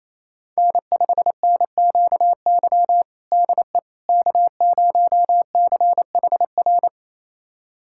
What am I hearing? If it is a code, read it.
Morse code, heard as a keyed tone, about 28 wpm, 700 Hz. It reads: N5NQY DE K0CHR
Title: USA Callsign DE Callsign - 5 Characters - 28wpm